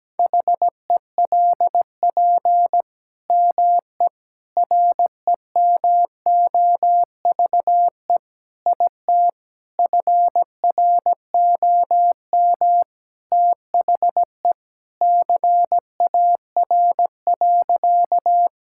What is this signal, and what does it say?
Morse code, heard as a keyed tone, about 17 wpm, 700 Hz. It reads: HELP ME REMOVE IT FROM THE CAR.
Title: Sentences from Top 1000 Words - Review Part 5 - 17wpm